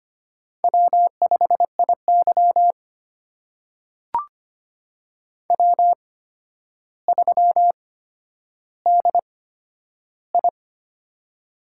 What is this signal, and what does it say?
Morse code, heard as a keyed tone, about 25 wpm, spaced as for 11 wpm, 700 Hz. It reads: W5IY W3DI
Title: USA Call Signs - 4 Characters - 8x Character Spacing - 25wpm